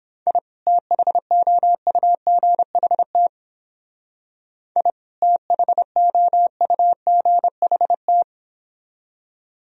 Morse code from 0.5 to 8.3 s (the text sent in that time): THOUGHT I THOUGHT